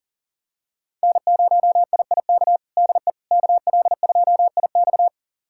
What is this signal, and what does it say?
Morse code, heard as a keyed tone, about 40 wpm, 700 Hz. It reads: N0IIK DE KL2IX
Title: USA Callsign DE Callsign - 5 Characters - 40wpm